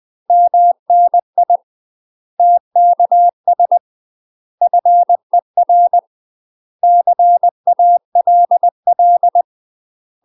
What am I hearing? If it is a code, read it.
Morse code, heard as a keyed tone, about 20 wpm, 700 Hz. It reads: MNI TKS FER CALL
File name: UK MASTER QSOs TRAINING / 20WPM / UK MASTER QSOs 0004